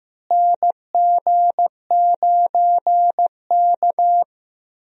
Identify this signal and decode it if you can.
Morse code, heard as a keyed tone, about 15 wpm, 700 Hz. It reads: NG9K
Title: USA Call Signs - 4 Characters - Repeated 2x - 15wpm